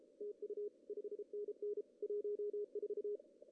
Morse code, heard as a keyed tone, about 35 wpm, 410 Hz. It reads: TU 5NN 14
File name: CW / cqww-cw-k5tr